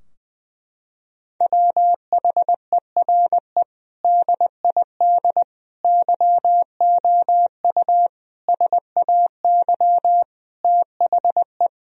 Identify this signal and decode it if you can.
Morse code, heard as a keyed tone, about 21 wpm, 700 Hz. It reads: WHERE DID YOU SAY THE
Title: Sentences from Top 1000 Words - Review Part 3 - 20wpm